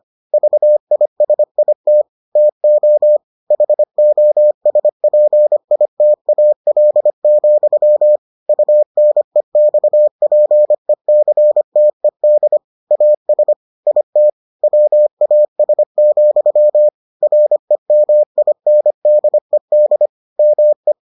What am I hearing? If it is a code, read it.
Morse code, heard as a keyed tone, about 25 wpm, 600 Hz. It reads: VISIT TO HOSPITAL, UNEXPECTED AS IT WAS, REMINDED ME